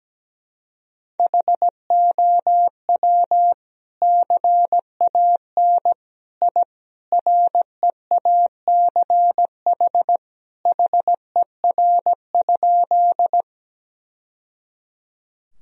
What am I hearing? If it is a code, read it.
Morse code, heard as a keyed tone, about 17 wpm, 700 Hz. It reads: HOW CAN I REACH HER?
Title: Sentences from Top 1000 Words - Review Part 2 - 17wpm